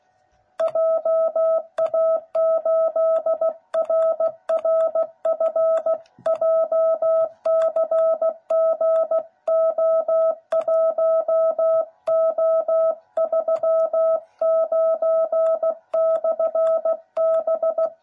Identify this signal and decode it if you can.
Morse code, heard as a keyed tone, about 16 wpm, 660 Hz. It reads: JA8RRF JCGO1O39/B
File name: jh8cbh / 2021cw / cw09b